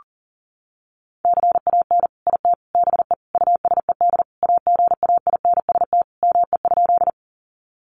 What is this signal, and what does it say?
Morse code, heard as a keyed tone, about 40 wpm, 700 Hz. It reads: CAN IT BE USED AGAINST ME?
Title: Sentences from Top 600 Words 40wpm